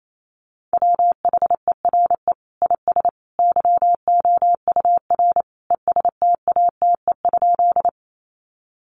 Morse code, heard as a keyed tone, about 28 wpm, 700 Hz. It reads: WHERE IS YOUR ESTATE?
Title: Sentences from Top 1200 Words 28wpm